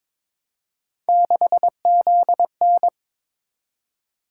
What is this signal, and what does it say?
Morse code, heard as a keyed tone, about 22 wpm, 700 Hz. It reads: 6ZN
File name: practice / 22wpm / Three Letter-Number Combinations - 22wpm